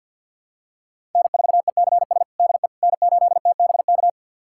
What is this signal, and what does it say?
Morse code, heard as a keyed tone, about 48 wpm, 700 Hz. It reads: N4ECS DE N8TBK
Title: USA Callsign DE Callsign - 5 Characters - 50wpm